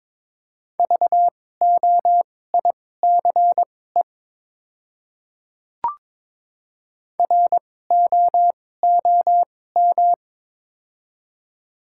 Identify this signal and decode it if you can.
Morse code, heard as a keyed tone, about 21 wpm, spaced as for 16 wpm, 690 Hz. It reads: VOICE ROOM
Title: 2x Character Spacing - Top 500 Words - 22wpm